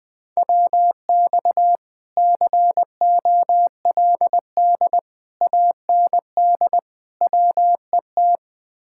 Morse code, heard as a keyed tone, about 21 wpm, 700 Hz. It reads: WX COLD AND WET
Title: Speed-Racing Common QSO Phrases Encore - 20wpm